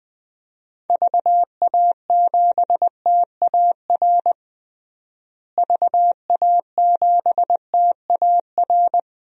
Morse code, heard as a keyed tone, about 20 wpm, 700 Hz. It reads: VA7TAR VA7TAR
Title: DXCC-100 Call Signs - 6 Characters - Repeated 3x - 20wpm